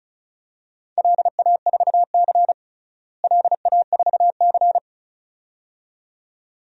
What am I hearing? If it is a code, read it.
Morse code, heard as a keyed tone, about 36 wpm, 700 Hz. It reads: LA4C LA4C
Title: DXCC-100 Call Signs - 4 Characters - Repeated 2x - 35wpm